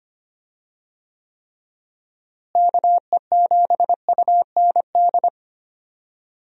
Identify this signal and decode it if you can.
Morse code, heard as a keyed tone, about 25 wpm, 700 Hz. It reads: KE7UND
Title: USA Call Signs - 6 Characters - Repeated 3x - 25wpm